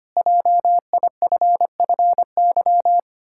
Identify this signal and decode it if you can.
Morse code, heard as a keyed tone, about 25 wpm, 700 Hz. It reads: JIFFY
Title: Pangram Challenge v2 - 25wpm